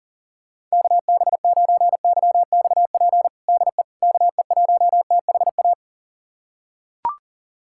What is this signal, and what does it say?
Morse code, heard as a keyed tone, about 40 wpm, 700 Hz. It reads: KB9YXP DE KE1THA
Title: USA Callsign DE Callsign 40wpm